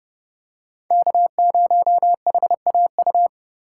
Morse code, heard as a keyed tone, about 30 wpm, 700 Hz. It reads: K0HAU
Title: USA Call Signs - 5 Characters - 3x Character Spacing - 30wpm